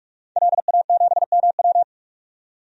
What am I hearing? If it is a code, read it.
Morse code, heard as a keyed tone, about 46 wpm, 700 Hz. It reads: LA7MW
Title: DXCC-100 Call Signs - 5 Characters - 45wpm